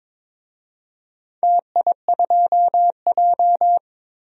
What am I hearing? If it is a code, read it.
Morse code, heard as a keyed tone, about 22 wpm, 700 Hz. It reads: TI2J
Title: DXCC-100 Call Signs - 4 Characters - 22wpm